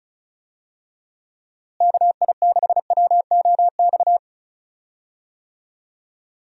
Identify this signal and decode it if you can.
Morse code, heard as a keyed tone, about 36 wpm, 700 Hz. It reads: KI6WOX
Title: USA Call Signs - 6 Characters - Repeated 2x - 35wpm